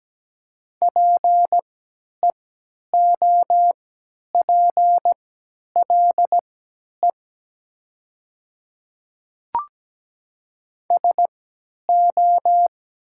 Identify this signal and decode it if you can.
Morse code, heard as a keyed tone, about 17 wpm, spaced as for 10 wpm, 700 Hz. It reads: PEOPLE SO
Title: 3x Character Spacing - Top 200 Words - 17wpm